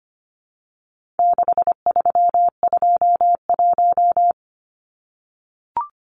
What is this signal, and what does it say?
Morse code, heard as a keyed tone, about 25 wpm, 700 Hz. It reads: 6321
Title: Numbers - 4-Digit 25wpm